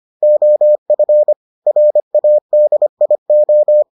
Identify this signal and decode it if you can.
Morse code, heard as a keyed tone, about 25 wpm, 600 Hz. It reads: OF RADIO